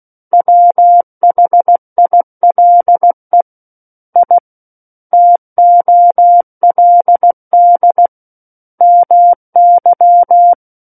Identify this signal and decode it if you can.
Morse code, heard as a keyed tone, about 16 wpm, 700 Hz. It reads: WHILE I TOLD MY
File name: Book One / 07-Chapter 7 - 16 WPM